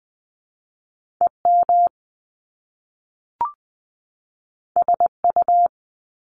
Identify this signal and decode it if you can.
Morse code, heard as a keyed tone, about 20 wpm, 700 Hz. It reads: EM SU
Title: Top Two Letter Combinations in Words 20wpm